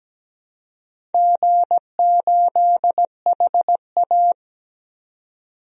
Morse code, heard as a keyed tone, about 17 wpm, 700 Hz. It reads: G8HA
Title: DXCC-100 Call Signs - 4 Characters - 17wpm